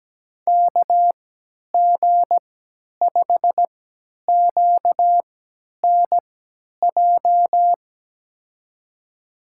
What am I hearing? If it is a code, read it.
Morse code, heard as a keyed tone, about 17 wpm, spaced as for 10 wpm, 700 Hz. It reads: KG5QNJ
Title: USA Call Signs - 3x Character Spacing - 17wpm